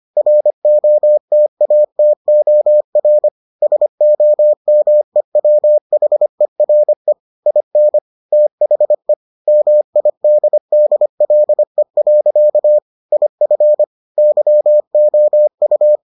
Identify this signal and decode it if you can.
Morse code, heard as a keyed tone, about 25 wpm, 600 Hz. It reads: ROTATOR SOMEWHERE IN THE MIDDLE. IF YOU